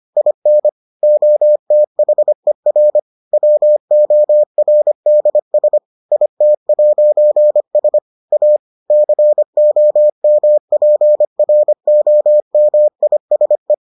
Morse code, heard as a keyed tone, about 25 wpm, 600 Hz. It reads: IN OTHER WORDS IT'S A COMPROMISE